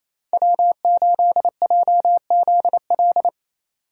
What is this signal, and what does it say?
Morse code, heard as a keyed tone, about 34 wpm, 700 Hz. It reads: W8JZL
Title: Speed Racing - USA 1x3 Call Signs 28wpm